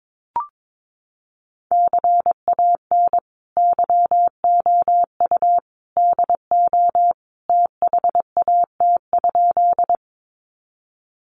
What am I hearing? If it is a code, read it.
Morse code, heard as a keyed tone, about 22 wpm, 700 Hz. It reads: CAN YOU DO THAT?
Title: Sentences from Top 100 Words 22wpm